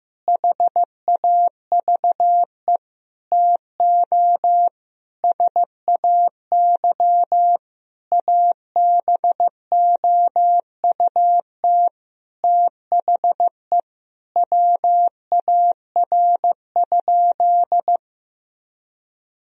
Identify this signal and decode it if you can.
Morse code, heard as a keyed tone, about 16 wpm, 700 Hz. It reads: HAVE TO SAY ABOUT THE WAR?
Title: Sentences from Top 1000 Words - Review Part 4 - 15wpm